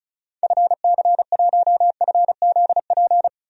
Morse code, heard as a keyed tone, about 36 wpm, 700 Hz. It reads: FC1FZP
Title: DXCC-100 Call Signs - 6 Characters - Repeated 3x - 35wpm